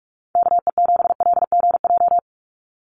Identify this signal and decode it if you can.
Morse code, heard as a keyed tone, about 44 wpm, 700 Hz. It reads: KE7LGJ
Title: USA Call Signs 45wpm